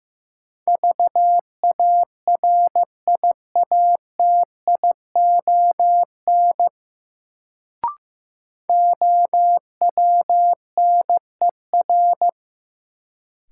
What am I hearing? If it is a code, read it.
Morse code, heard as a keyed tone, about 16 wpm, 700 Hz. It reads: VARIATION OWNER
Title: Sentences from Top 1600 Words - New Word Drill - 15wpm